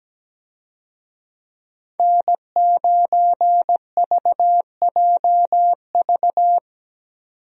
Morse code, heard as a keyed tone, about 17 wpm, 700 Hz. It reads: N9VJV
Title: USA Call Signs - 5 Characters - 17wpm